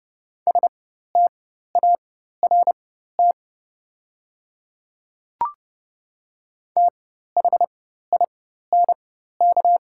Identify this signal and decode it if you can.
Morse code, heard as a keyed tone, about 30 wpm, spaced as for 14 wpm, 700 Hz. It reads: START THINK
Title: 4x Character Spacing - Top 300 Words - 30wpm